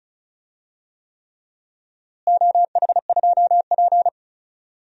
Morse code, heard as a keyed tone, about 36 wpm, 700 Hz. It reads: OH2P
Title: DXCC-100 Call Signs - 4 Characters - Repeated 3x - 35wpm